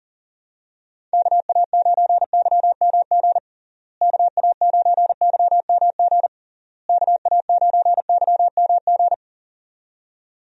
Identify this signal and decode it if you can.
Morse code, heard as a keyed tone, about 37 wpm, 700 Hz. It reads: KA9YMG KA9YMG KA9YMG
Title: USA Call Signs - 6 Characters - Repeated 3x - 40wpm